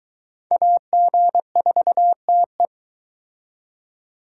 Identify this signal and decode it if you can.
Morse code, heard as a keyed tone, about 23 wpm, 700 Hz. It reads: AG4TE